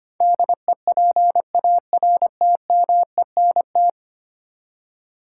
Morse code, heard as a keyed tone, about 25 wpm, 700 Hz. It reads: DEPARTMENT
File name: Ten Letter Words - 25wpm